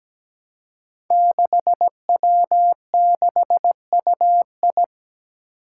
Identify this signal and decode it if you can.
Morse code, heard as a keyed tone, about 17 wpm, 700 Hz. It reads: 6W6UI
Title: Five Challenging Letter-Number Combinations 17wpm